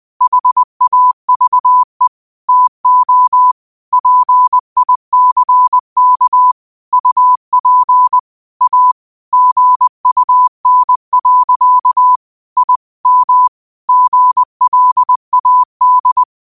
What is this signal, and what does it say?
Morse code, heard as a keyed tone, about 20 wpm, 1000 Hz. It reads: HAVE TO PICK UP A GUN. I M GLAD
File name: qotdfiles / 20wpm